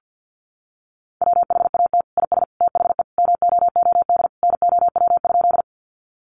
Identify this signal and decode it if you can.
Morse code, heard as a keyed tone, about 50 wpm, 700 Hz. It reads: WHAT IS THE MOOD NOW?
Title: Sentences from Top 1100 Words 50wpm